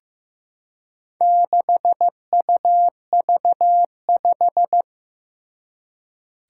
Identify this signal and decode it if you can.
Morse code, heard as a keyed tone, about 15 wpm, 700 Hz. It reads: 6UV5
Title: Four Challenging Letter-Number Combinations 15wpm